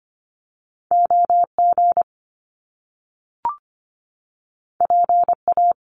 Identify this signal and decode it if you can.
Morse code, heard as a keyed tone, about 25 wpm, 700 Hz. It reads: OG PA
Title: Top Two Letter Combinations in Words 25wpm